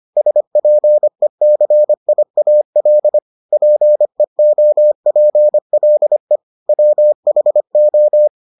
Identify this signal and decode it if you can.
Morse code, heard as a keyed tone, about 25 wpm, 600 Hz. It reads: SPECIAL PEOPLE WHO